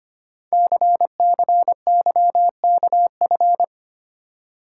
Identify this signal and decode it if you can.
Morse code, heard as a keyed tone, about 25 wpm, 700 Hz. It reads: CCYKF
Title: Code Groups - 5 Letters - 25wpm